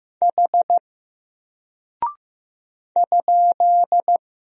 Morse code, heard as a keyed tone, about 15 wpm, 700 Hz. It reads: H ?
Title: Single Character (Letter:Number:Punctuation:Prosign) Alt 15wpm